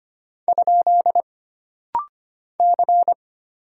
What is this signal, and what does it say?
Morse code, heard as a keyed tone, about 25 wpm, 700 Hz. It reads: ? C
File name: Single Character (Letter:Number:Punctuation:Prosign) - Rapid-Fire - 25wpm